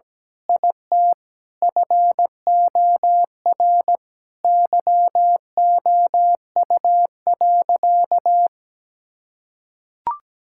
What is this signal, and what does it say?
Morse code, heard as a keyed tone, about 17 wpm, 700 Hz. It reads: IT FOR YOU.
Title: Sentences from Top 1000 Words - Review Part 6 - 17wpm